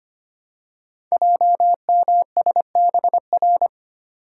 Morse code, heard as a keyed tone, about 25 wpm, 700 Hz. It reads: JMSBR
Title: Code Groups - 5 Letters - 25wpm